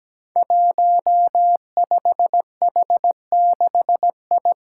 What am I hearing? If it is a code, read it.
Morse code, heard as a keyed tone, about 17 wpm, 700 Hz. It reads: 15H6I
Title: Five Challenging Letter-Number Combinations 17wpm